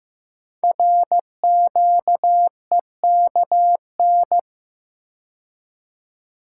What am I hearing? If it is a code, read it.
Morse code, heard as a keyed tone, about 15 wpm, 700 Hz. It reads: RQEKN
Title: Code Groups - 5 Letters - 15wpm